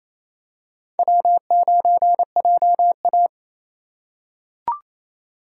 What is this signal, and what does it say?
Morse code, heard as a keyed tone, about 28 wpm, 700 Hz. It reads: W9JA
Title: USA Call Signs - 4 Characters - 3x Character Spacing - 28wpm